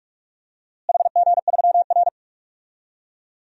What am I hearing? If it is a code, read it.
Morse code, heard as a keyed tone, about 46 wpm, 700 Hz. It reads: HG3R
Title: DXCC-100 Call Signs - 4 Characters - Repeated 2x - 45wpm